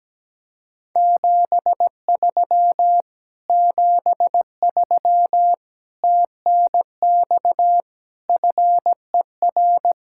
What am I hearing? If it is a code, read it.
Morse code, heard as a keyed tone, about 17 wpm, 700 Hz. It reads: 73 73 TNX FER
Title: Speed-Racing Common QSO Phrases Encore - 17wpm